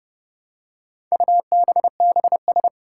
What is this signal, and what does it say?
Morse code, heard as a keyed tone, about 31 wpm, 700 Hz. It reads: UBBS
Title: Four Challenging Letter-Number Combinations 30wpm